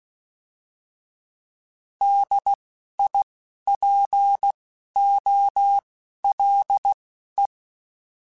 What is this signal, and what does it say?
Morse code, heard as a keyed tone, about 16 wpm, spaced as for 12 wpm, 800 Hz. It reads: DIPOLE
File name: fromSMs / FromSMs mp3s / Antennas@10